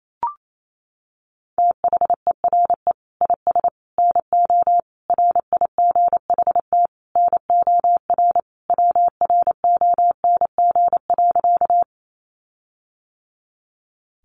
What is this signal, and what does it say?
Morse code, heard as a keyed tone, about 28 wpm, 700 Hz. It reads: THERE IS NO RIGHT NOR WRONG.
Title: Sentences from Top 1500 Words 28wpm